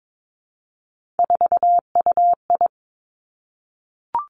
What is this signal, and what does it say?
Morse code, heard as a keyed tone, about 22 wpm, 700 Hz. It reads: 4UI
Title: Three Challenging Letter-Number Combinations 22wpm